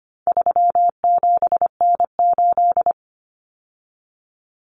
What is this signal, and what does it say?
Morse code, heard as a keyed tone, about 25 wpm, 700 Hz. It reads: 37N8
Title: Common Cut Numbers - 4-Digit 25wpm